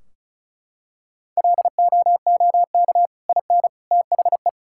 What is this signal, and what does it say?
Morse code, heard as a keyed tone, about 36 wpm, 700 Hz. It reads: LOOK IN THE
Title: Sentences from Top 1600 Words 35wpm